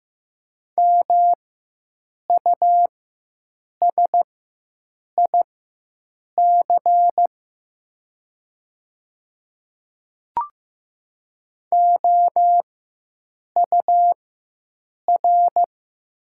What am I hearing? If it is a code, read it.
Morse code, heard as a keyed tone, about 15 wpm, spaced as for 7 wpm, 700 Hz. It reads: MUSIC OUR
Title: 4x Character Spacing - Top 300 Words - 15wpm